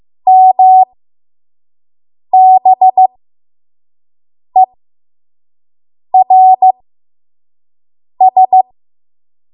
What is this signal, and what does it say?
Morse code, heard as a keyed tone, about 15 wpm, spaced as for 5 wpm, 750 Hz. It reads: MBERS
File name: Morse / 05WPM